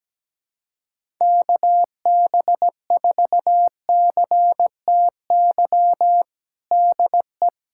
Morse code, heard as a keyed tone, about 17 wpm, 700 Hz. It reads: KB4CTY DE
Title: USA Callsign DE Callsign 17wpm